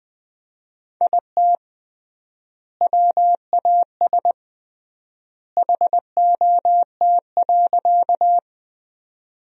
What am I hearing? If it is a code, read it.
Morse code, heard as a keyed tone, about 20 wpm, 700 Hz. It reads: IT WAS HOT.
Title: 3x Word-Spacing - Sentences from Top 500 Words - 20wpm